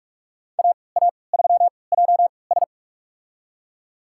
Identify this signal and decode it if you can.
Morse code, heard as a keyed tone, about 46 wpm, spaced as for 26 wpm, 700 Hz. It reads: AA3JS
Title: USA Call Signs - 3x Character Spacing - 45wpm